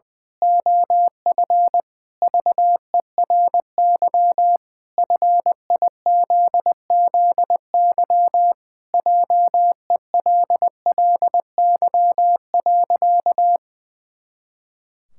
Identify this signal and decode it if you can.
Morse code, heard as a keyed tone, about 21 wpm, 700 Hz. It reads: OF VERY FIZZY JELLY.
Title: Pangram Challenge v2 - 20wpm